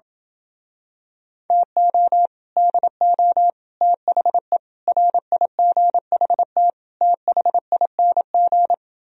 Morse code, heard as a keyed tone, about 27 wpm, 700 Hz. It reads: TO DO THE RIGHT THING